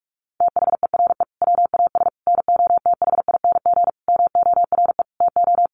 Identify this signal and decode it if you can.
Morse code, heard as a keyed tone, about 46 wpm, 700 Hz. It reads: THERE WAS NOTHING MORE TO
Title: Sentences from Top 1100 Words 45wpm